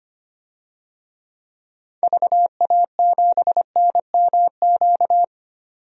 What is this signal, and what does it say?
Morse code, heard as a keyed tone, about 25 wpm, 700 Hz. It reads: VA7NMQ
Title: DXCC-100 Call Signs - 6 Characters - 25wpm